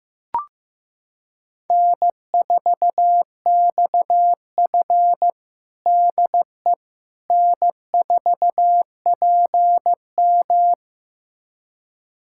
Morse code, heard as a keyed tone, about 15 wpm, 700 Hz. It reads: N4XF DE N4PM
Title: USA Callsign DE Callsign - 4 Characters - 15wpm